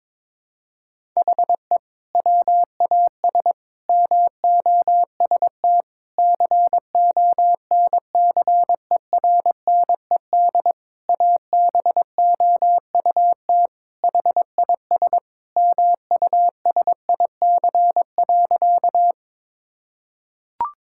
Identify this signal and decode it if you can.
Morse code, heard as a keyed tone, about 22 wpm, 700 Hz. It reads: HE WAS MOST CONCERNED ABOUT HIS MUSIC.
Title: Sentences from Top 1000 Words - Review Part 6 - 22wpm